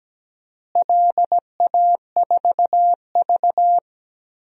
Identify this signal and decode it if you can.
Morse code, heard as a keyed tone, about 17 wpm, 700 Hz. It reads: LA4V
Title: DXCC-100 Call Signs - 4 Characters - Repeated 2x - 17wpm